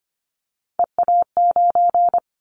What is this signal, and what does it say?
Morse code, heard as a keyed tone, about 25 wpm, 700 Hz. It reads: EA9
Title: Callsign Prefixes for Top 100 DXCC - 25wpm